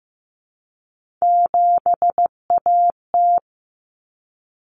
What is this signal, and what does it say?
Morse code, heard as a keyed tone, about 15 wpm, 700 Hz. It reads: 7AT
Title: Common Cut Numbers - 3-Digit 15wpm